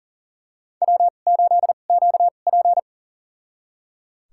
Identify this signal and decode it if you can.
Morse code, heard as a keyed tone, about 38 wpm, spaced as for 28 wpm, 700 Hz. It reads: W8QP
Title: USA Call Signs - 4 Characters - 2x Character Spacing - 40wpm